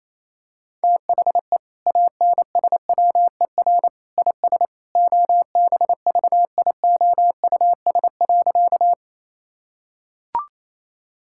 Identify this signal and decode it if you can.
Morse code, heard as a keyed tone, about 28 wpm, 700 Hz. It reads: THE ANSWER IS OBVIOUS.
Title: Sentences from Top 1600 Words 28wpm